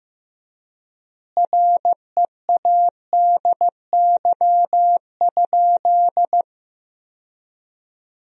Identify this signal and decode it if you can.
Morse code, heard as a keyed tone, about 15 wpm, 700 Hz. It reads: READY?
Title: 3x Word-Spacing - Sentences from Top 400 Words - 15wpm